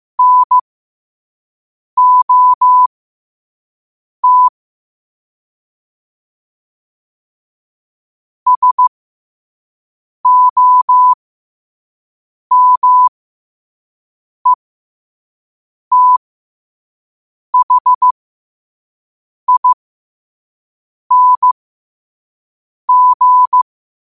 Morse code, heard as a keyed tone, about 15 wpm, spaced as for 5 wpm, 1000 Hz. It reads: NOT SOMETHING
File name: qotdfiles / 5wpm